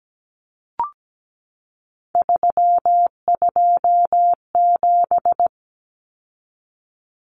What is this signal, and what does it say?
Morse code, heard as a keyed tone, about 17 wpm, 700 Hz. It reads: 327
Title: Common Cut Numbers - 3-Digit 17wpm